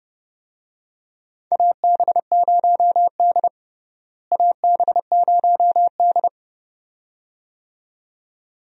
Morse code, heard as a keyed tone, about 30 wpm, 700 Hz. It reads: AB0D AB0D
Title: USA Call Signs - 4 Characters - Repeated 2x - 30wpm